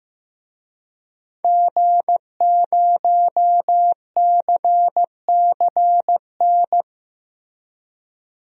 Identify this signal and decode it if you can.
Morse code, heard as a keyed tone, about 15 wpm, 700 Hz. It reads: G0CCN
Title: DXCC-100 Call Signs - 5 Characters - Repeated 3x - 15wpm